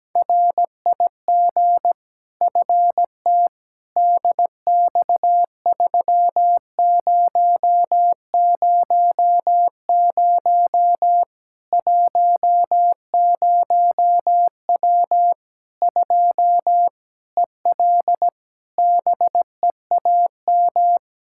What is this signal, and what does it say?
Morse code, heard as a keyed tone, about 17 wpm, 700 Hz. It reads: RIG FT DX3000 10W 2 EL BEAM
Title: Practice QSO - Exchange 2 - Info - 17wpm